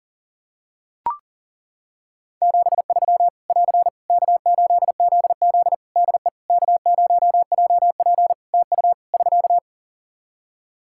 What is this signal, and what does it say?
Morse code, heard as a keyed tone, about 40 wpm, 700 Hz. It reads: 73 <AR> K8ZZ DE K0JP TU <SK>
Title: Practice QSO - Exchange 4 - Ending - 40wpm